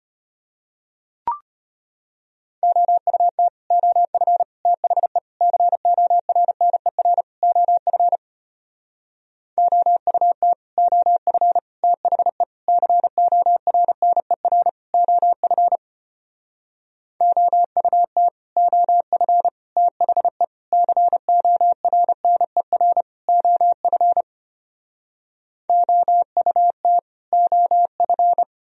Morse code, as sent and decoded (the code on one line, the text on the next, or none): --- ..- - / --- ..-. / - .... / -.-. --- .-. -. .-. / --- ..-. / --- ..- - / --- ..-. / - .... / -.-. --- .-. -. .-. / --- ..-. / --- ..- - / --- ..-. / - .... / -.-. --- .-. -. .-. / --- ..-. / --- ..- - / --- ..-.
OUT OF THE CORNER OF OUT OF THE CORNER OF OUT OF THE CORNER OF OUT OF